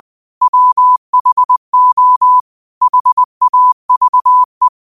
Morse code, heard as a keyed tone, about 20 wpm, 1000 Hz. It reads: WHO HAVE
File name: qotdfiles / 20wpm